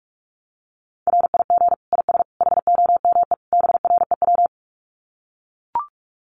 Morse code, heard as a keyed tone, about 44 wpm, 700 Hz. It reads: RIG IS HOME BREW
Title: Common QSO Phrases 45wpm